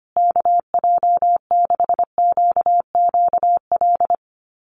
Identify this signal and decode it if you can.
Morse code, heard as a keyed tone, about 25 wpm, 700 Hz. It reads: KJ6QQL
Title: USA Call Signs Repeated 3x 25wpm